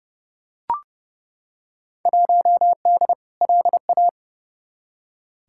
Morse code, heard as a keyed tone, about 30 wpm, 700 Hz. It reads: 1D LA
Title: Field Day Exchanges - 30wpm